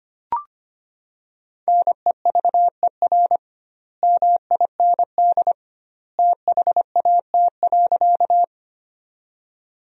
Morse code, heard as a keyed tone, about 25 wpm, 700 Hz. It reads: NEVER MIND THAT.
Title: 2x Word-Spacing - Sentences from Top 500 Words - 25wpm